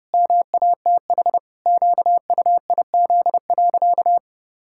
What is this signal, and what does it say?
Morse code, heard as a keyed tone, about 30 wpm, 700 Hz. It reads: MATH QUIZ.
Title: Pangram Challenge v2 - 30wpm